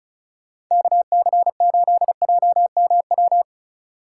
Morse code, heard as a keyed tone, about 39 wpm, 700 Hz. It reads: KC8JMW